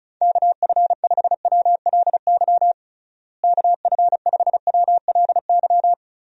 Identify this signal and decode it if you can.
Morse code, heard as a keyed tone, about 34 wpm, 700 Hz. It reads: KF5WLY KF5WLY
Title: USA Call Signs - 6 Characters - Repeated 3x - 35wpm